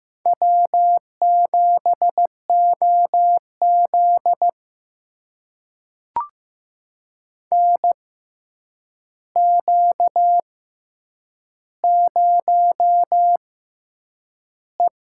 Morse code, heard as a keyed tone, about 15 wpm, spaced as for 8 wpm, 700 Hz. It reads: W7OZ NQ0E
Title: USA Call Signs - 4 Characters - 6x Character Spacing - 15wpm